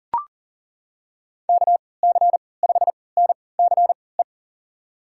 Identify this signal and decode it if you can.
Morse code, heard as a keyed tone, about 38 wpm, spaced as for 22 wpm, 700 Hz. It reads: KC5NCE
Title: USA Call Signs - 6 Characters - 3x Character Spacing - 40wpm